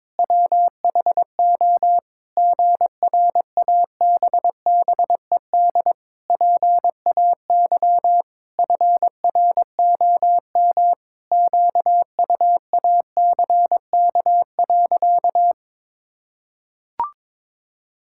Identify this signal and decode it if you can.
Morse code, heard as a keyed tone, about 22 wpm, 700 Hz. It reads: WHO GRABBED PAY FROM QUACK.
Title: Pangram Challenge v2 - 22wpm